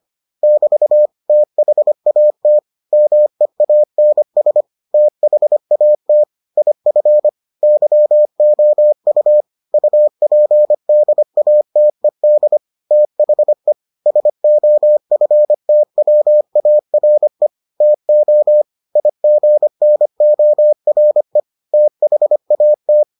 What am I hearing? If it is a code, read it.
Morse code, heard as a keyed tone, about 25 wpm, 600 Hz. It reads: <BT> THAT MEANS THAT IF YOU UPDATED THE SOFTWARE TO IGNORE THAT